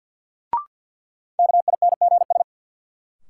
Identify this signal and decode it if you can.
Morse code, heard as a keyed tone, about 46 wpm, 700 Hz. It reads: KINGS
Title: Sentences from Top 1000 Words - Review Part 6 - New Word Drill - 50wpm